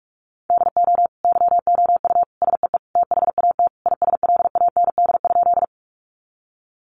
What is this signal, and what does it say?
Morse code, heard as a keyed tone, about 44 wpm, 700 Hz. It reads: DO YOU SEE THAT ISLAND?
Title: Sentences from Top 500 Words 45wpm